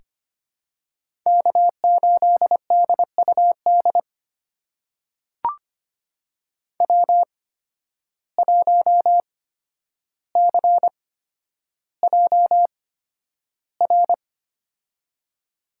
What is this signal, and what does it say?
Morse code, heard as a keyed tone, about 25 wpm, spaced as for 11 wpm, 700 Hz. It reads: K8DUD W1CJR
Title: USA Call Signs - 5 Characters - 8x Character Spacing - 25wpm